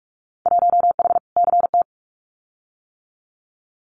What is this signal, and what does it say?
Morse code, heard as a keyed tone, about 44 wpm, 700 Hz. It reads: 1H CT
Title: Winter Field Day Exchanges 45wpm